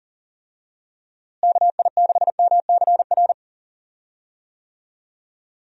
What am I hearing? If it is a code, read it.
Morse code, heard as a keyed tone, about 40 wpm, 700 Hz. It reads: KI6MCR
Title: USA Call Signs - 6 Characters - Repeated 2x - 40wpm